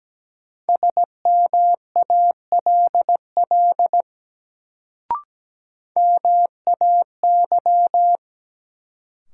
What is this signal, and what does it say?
Morse code, heard as a keyed tone, about 17 wpm, 700 Hz. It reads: SMALL MAY
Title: Sets of 2 Words - New Word Drill - 17wpm